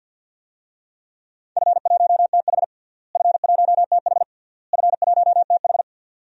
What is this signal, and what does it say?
Morse code, heard as a keyed tone, about 53 wpm, 700 Hz. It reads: F1TH F1TH F1TH
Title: DXCC-100 Call Signs - 4 Characters - Repeated 3x - 50wpm